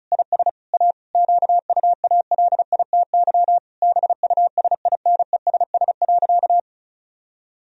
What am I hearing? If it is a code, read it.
Morse code, heard as a keyed tone, about 34 wpm, 700 Hz. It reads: IS A QUALITY BUSINESS.
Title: Sentences from Top 1000 Words - Review Part 4 - 35wpm